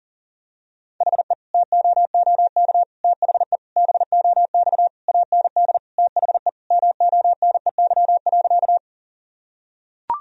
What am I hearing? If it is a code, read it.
Morse code, heard as a keyed tone, about 40 wpm, 700 Hz. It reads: HE TOOK THE BOX AND THE MONEY.
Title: Sentences from Top 1000 Words - Review Part 2 - 40wpm